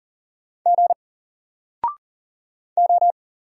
Morse code, heard as a keyed tone, about 40 wpm, 700 Hz. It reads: G O
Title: Single Letter-Number Alt 40wpm